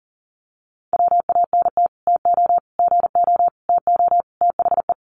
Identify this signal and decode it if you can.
Morse code, heard as a keyed tone, about 40 wpm, 700 Hz. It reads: WANT TO GO TO THE
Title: Sets of 5 Words Encore2 40wpm